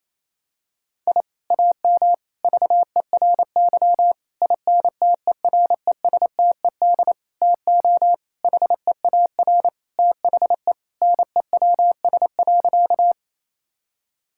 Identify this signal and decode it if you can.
Morse code, heard as a keyed tone, about 28 wpm, 700 Hz. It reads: I AM VERY INTERESTED TO HEAR THE NEWS.
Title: Sentences from Top 1000 Words - Review Part 6 - 28wpm